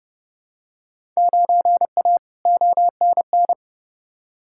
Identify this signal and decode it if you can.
Morse code, heard as a keyed tone, about 30 wpm, 700 Hz. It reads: 9A ONN
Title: Field Day Exchanges Encore - 30wpm